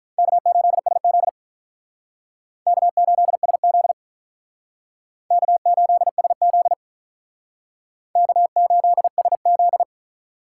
Morse code, heard as a keyed tone, about 44 wpm, 700 Hz. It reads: K8SZ K8SZ K8SZ K8SZ